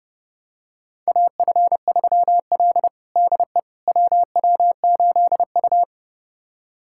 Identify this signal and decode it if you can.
Morse code, heard as a keyed tone, about 30 wpm, 700 Hz. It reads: AF3L DE WW8U
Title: USA Callsign DE Callsign - 4 Characters - 30wpm